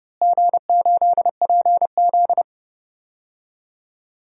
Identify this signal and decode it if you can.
Morse code, heard as a keyed tone, about 30 wpm, 700 Hz. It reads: G8PZ
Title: DXCC-100 Call Signs - 4 Characters - Repeated 3x - 30wpm